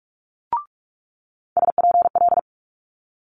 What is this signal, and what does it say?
Morse code, heard as a keyed tone, about 46 wpm, 700 Hz. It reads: SPL
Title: Multi-letter Phonemes Quick Fire 45wpm